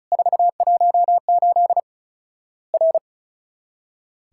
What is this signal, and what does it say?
Morse code, heard as a keyed tone, about 36 wpm, 700 Hz. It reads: 418 R
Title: Worked All Europe DX Contest - QTC Exchange - 35wpm